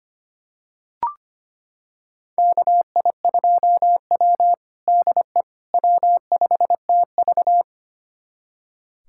Morse code, heard as a keyed tone, about 25 wpm, 700 Hz. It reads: KI2W DE W5TV
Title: USA Callsign DE Callsign - 4 Characters - 25wpm